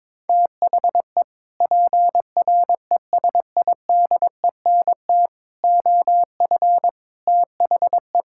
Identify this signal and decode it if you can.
Morse code, heard as a keyed tone, about 22 wpm, 700 Hz. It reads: THE PRESIDENT OF THE
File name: practice / 22wpm / Sentences with 4-Word Phrases - 22wpm